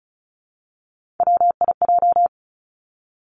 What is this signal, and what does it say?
Morse code, heard as a keyed tone, about 36 wpm, 700 Hz. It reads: WIJ
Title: Three Challenging Letter-Number Combinations 35wpm